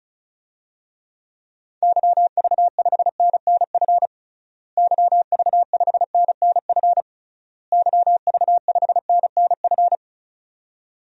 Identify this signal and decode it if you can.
Morse code, heard as a keyed tone, about 34 wpm, 700 Hz. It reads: YV5NNF YV5NNF YV5NNF
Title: DXCC-100 Call Signs - 6 Characters - Repeated 3x - 35wpm